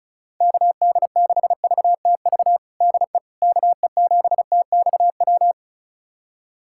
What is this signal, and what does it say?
Morse code, heard as a keyed tone, about 34 wpm, 700 Hz. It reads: KD6VTV DE KE7TXW
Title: USA Callsign DE Callsign 35wpm